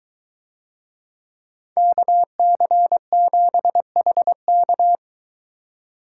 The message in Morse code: KC7HK

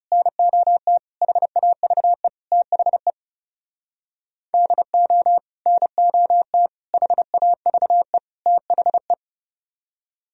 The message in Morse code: NOT HAVE THE DO NOT HAVE THE

